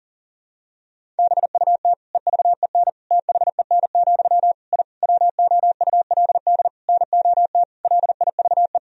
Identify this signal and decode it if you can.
Morse code, heard as a keyed tone, about 40 wpm, 700 Hz. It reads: BUT EVEN THEN, I WOULD NOT LIVE
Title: Sentences from Top 1000 Words - Review Part 1 - 40wpm